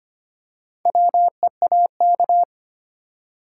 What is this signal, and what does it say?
Morse code, heard as a keyed tone, about 25 wpm, 700 Hz. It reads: WEAK
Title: Four Letter Words - 25wpm